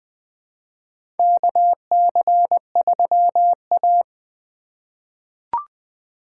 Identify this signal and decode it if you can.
Morse code, heard as a keyed tone, about 20 wpm, 700 Hz. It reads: KC3A
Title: USA Call Signs - 4 Characters - 4x Character Spacing - 20wpm